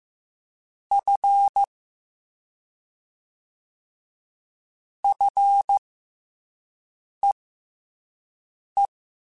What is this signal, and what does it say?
Morse code, heard as a keyed tone, about 15 wpm, spaced as for 5 wpm, 800 Hz. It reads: F FEE
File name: F and L / F and L mp3 / F and L Prefixes 3x2 Twice @5